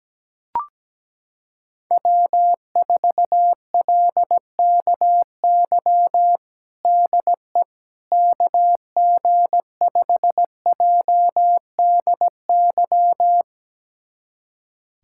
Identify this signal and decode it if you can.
Morse code, heard as a keyed tone, about 17 wpm, 700 Hz. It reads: W4LKY DE KG5JDY
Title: USA Callsign DE Callsign 17wpm